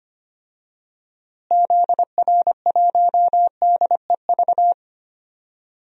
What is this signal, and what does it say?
Morse code, heard as a keyed tone, about 25 wpm, 690 Hz. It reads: ZR1DEV
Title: DXCC-100 Call Signs - 6 Characters - Repeated 3x - 25wpm